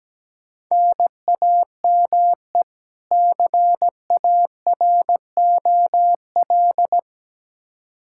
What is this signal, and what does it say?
Morse code, heard as a keyed tone, about 17 wpm, 700 Hz. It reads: NAME CAROL
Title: Practice QSO - Exchange 1 - Beginning - 17wpm